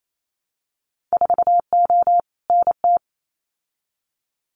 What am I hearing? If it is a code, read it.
Morse code, heard as a keyed tone, about 28 wpm, 700 Hz. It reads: 4O NT
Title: Winter Field Day Exchanges Encore 28wpm